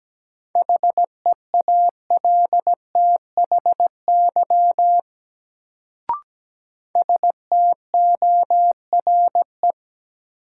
Morse code, heard as a keyed tone, about 17 wpm, 700 Hz. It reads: HEALTHY STORE
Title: Sentences from Top 1000 Words - Review Part 5 - New Word Drill - 17wpm